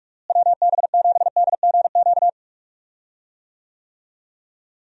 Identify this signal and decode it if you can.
Morse code, heard as a keyed tone, about 46 wpm, 700 Hz. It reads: WB7DGQ
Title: USA Call Signs - 6 Characters - Repeated 2x - 45wpm